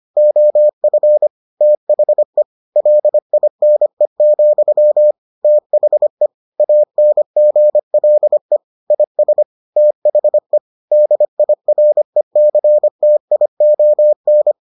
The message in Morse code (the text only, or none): OF THE LINE, THE ANGLE IS THE DIRECTION